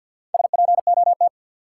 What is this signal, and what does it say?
Morse code, heard as a keyed tone, about 53 wpm, 700 Hz. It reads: SPOT
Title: Sentences from Top 1100 Words - New Word Drill Alt - 50wpm